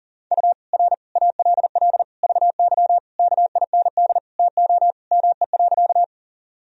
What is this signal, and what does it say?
Morse code, heard as a keyed tone, about 40 wpm, 700 Hz. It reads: U R ALL VY KIND TO ME.
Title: Sentences from Top 200 Words and Abbreviations 40wpm